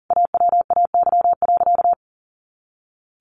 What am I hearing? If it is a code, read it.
Morse code, heard as a keyed tone, about 40 wpm, 700 Hz. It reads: AWAY.
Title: A Sound Of Thunder 40wpm